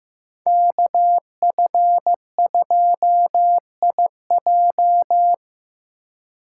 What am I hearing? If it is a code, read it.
Morse code, heard as a keyed tone, about 18 wpm, 700 Hz. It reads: KF2IJ
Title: Speed Racing - USA 2x2 Call Signs 15wpm